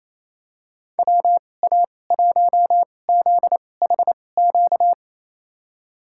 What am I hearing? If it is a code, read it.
Morse code, heard as a keyed tone, about 28 wpm, spaced as for 20 wpm, 690 Hz. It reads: WA1ZHQ
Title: USA Call Signs - 6 Characters - 2x Character Spacing - 28wpm